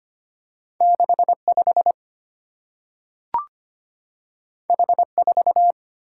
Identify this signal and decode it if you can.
Morse code, heard as a keyed tone, about 25 wpm, 690 Hz. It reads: 65 H4
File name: Two Letter-Number Combinations - 25wpm